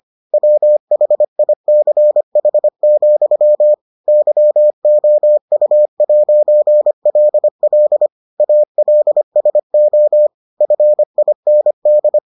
Morse code, heard as a keyed tone, about 25 wpm, 600 Hz. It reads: WHICH, YOU'LL ALSO FIND